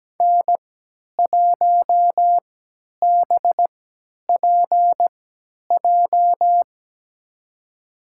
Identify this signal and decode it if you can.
Morse code, heard as a keyed tone, about 17 wpm, spaced as for 10 wpm, 700 Hz. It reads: N1BPJ
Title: USA Call Signs - 3x Character Spacing - 17wpm